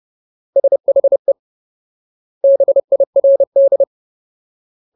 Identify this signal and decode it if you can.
Morse code, heard as a keyed tone, about 30 wpm, 550 Hz. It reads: SHE BIRD
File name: Ultimate Word List / 30WPM Ultimate